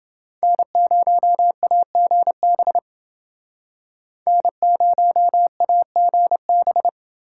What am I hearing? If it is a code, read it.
Morse code, heard as a keyed tone, about 28 wpm, 700 Hz. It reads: N0AGB N0AGB